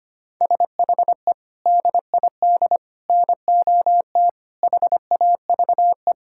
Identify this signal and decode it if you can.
Morse code, heard as a keyed tone, about 25 wpm, 700 Hz. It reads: SHE DID NOT HAVE
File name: practice / 25wpm / Sentences with 4-Word Phrases - 25wpm